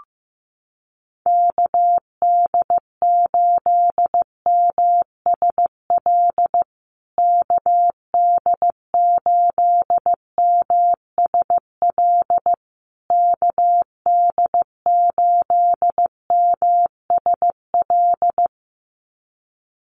Morse code, heard as a keyed tone, about 15 wpm, 700 Hz. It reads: KD8MSL KD8MSL KD8MSL
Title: USA Call Signs Repeated 3x 15wpm